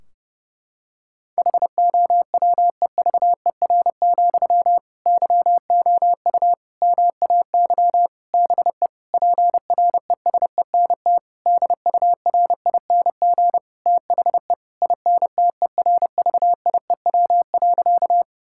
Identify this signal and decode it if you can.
Morse code, heard as a keyed tone, about 30 wpm, 700 Hz. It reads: HOWEVER, YOU MAY BE PRESENT DURING THE INTERVIEW.
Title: Sentences from Top 1600 Words 30wpm